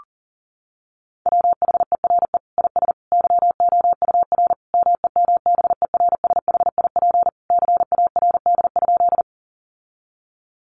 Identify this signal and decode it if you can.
Morse code, heard as a keyed tone, about 40 wpm, 700 Hz. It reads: WHERE IS YOUR MEMBERSHIP CARD?
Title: Sentences from Top 1400 Words 40wpm